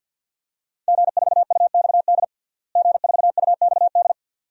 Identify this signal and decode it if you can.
Morse code, heard as a keyed tone, about 50 wpm, 700 Hz. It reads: G4UXD G4UXD
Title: DXCC-100 Call Signs - 5 Characters - Repeated 2x - 50wpm